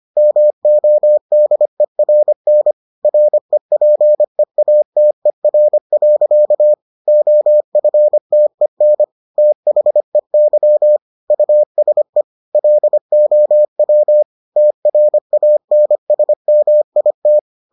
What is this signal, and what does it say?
Morse code, heard as a keyed tone, about 25 wpm, 600 Hz. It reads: MODERN REPEATER. OFTEN THEY USE LOW TRANSMIT